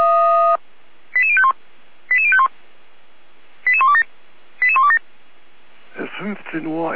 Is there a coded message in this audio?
Yes: five-tone selective call (ZVEI) from 1.1 to 5.0 s, heard as quick runs of five tones, tones of about 70 ms, with speech. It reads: 88051 88051 88127 88127